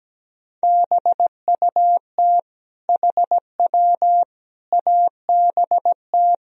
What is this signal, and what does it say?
Morse code, heard as a keyed tone, about 17 wpm, 700 Hz. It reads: BUT HW ABT
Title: Sentences from Top 100 Words and Abbreviations 17wpm